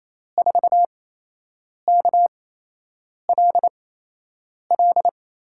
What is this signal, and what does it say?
Morse code, heard as a keyed tone, about 28 wpm, spaced as for 8 wpm, 700 Hz. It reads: 4KLL
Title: USA Call Signs - 6 Characters - 8x Character Spacing - 28wpm